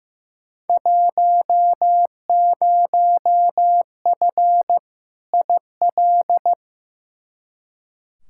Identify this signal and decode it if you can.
Morse code, heard as a keyed tone, about 15 wpm, 700 Hz. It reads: 10F IL
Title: Field Day Exchanges Encore - 15wpm